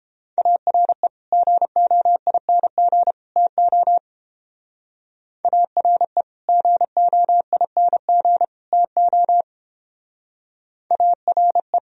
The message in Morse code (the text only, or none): ARE GOING TO ARE GOING TO ARE